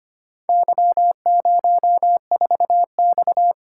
Morse code, heard as a keyed tone, about 25 wpm, 700 Hz. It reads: Y04X
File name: practice / 25wpm / Four Letter-Number Combinations - 25wpm